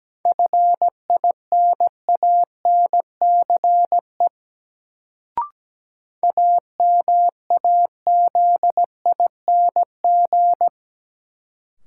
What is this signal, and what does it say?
Morse code, heard as a keyed tone, about 17 wpm, 700 Hz. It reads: FINANCE AMAZING
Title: Sentences from Top 1500 Words - New Word Drill Alt - 17wpm